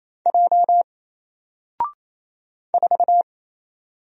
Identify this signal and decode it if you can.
Morse code, heard as a keyed tone, about 27 wpm, 700 Hz. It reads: J 4
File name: Single-Letter-Number-28wpm